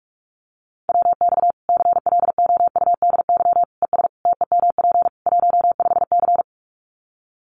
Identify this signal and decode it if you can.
Morse code, heard as a keyed tone, about 44 wpm, 700 Hz. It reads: WX CLOUDY ES TEMP 15C
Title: Common QSO Phrases 45wpm